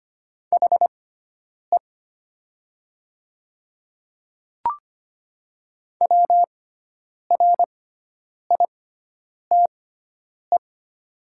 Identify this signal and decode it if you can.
Morse code, heard as a keyed tone, about 25 wpm, spaced as for 9 wpm, 700 Hz. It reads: HE WRITE
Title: 6x Character Spacing - Top 200 Words - 25wpm